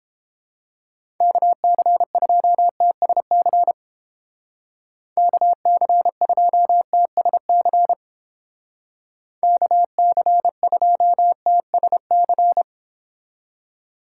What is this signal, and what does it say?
Morse code, heard as a keyed tone, about 29 wpm, 700 Hz. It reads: KC2TSC KC2TSC KC2TSC